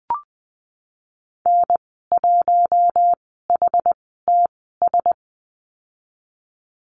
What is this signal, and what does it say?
Morse code, heard as a keyed tone, about 20 wpm, spaced as for 15 wpm, 700 Hz. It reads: N1HTS
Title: 088 Lesson 15g - Number 5 - Call Signs 20wpm